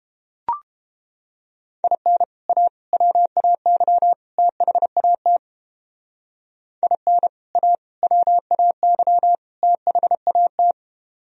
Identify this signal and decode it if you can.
Morse code, heard as a keyed tone, about 32 wpm, 700 Hz. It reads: IN A WAY THAT IN A WAY THAT